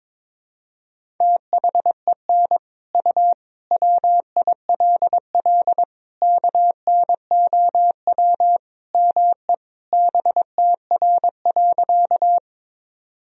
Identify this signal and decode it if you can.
Morse code, heard as a keyed tone, about 22 wpm, 700 Hz. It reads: THEN U WILL KNOW ME BTR.
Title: Sentences from Top 400 Words and Abbreviations 22wpm